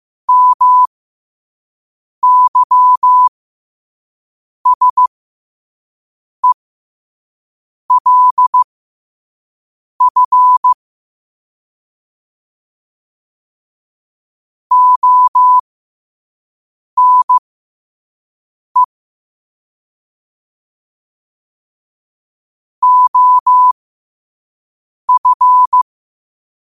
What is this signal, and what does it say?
Morse code, heard as a keyed tone, about 15 wpm, spaced as for 5 wpm, 1000 Hz. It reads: MYSELF ONE OF
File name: qotdfiles / 5wpm